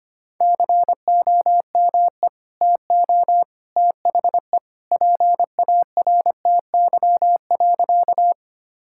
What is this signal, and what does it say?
Morse code, heard as a keyed tone, about 25 wpm, 700 Hz. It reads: COME TO THE PARTY.
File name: practice / 25wpm / Sentences with 2-Word Phrases - 25wpm